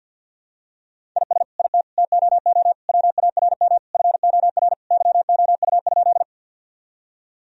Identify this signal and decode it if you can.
Morse code, heard as a keyed tone, about 52 wpm, 700 Hz. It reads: IS IT TOO WARM FOR YOU?
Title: Sentences from Top 1000 Words - Review Part 2 - 50wpm